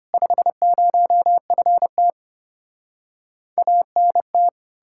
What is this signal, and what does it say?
Morse code, heard as a keyed tone, about 33 wpm, 700 Hz. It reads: 50FT ANT